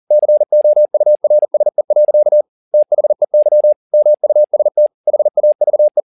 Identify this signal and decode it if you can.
Morse code, heard as a keyed tone, about 40 wpm, 600 Hz. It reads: COURSE. THEY MUST HAVE
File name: the old man and the sea 2 40wpm